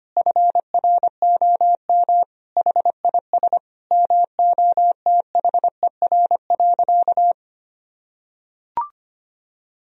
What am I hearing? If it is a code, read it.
Morse code, heard as a keyed tone, about 25 wpm, 700 Hz. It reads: FROM HIS MOTHER.
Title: Sentences from Top 1000 Words - Review Part 4 - 25wpm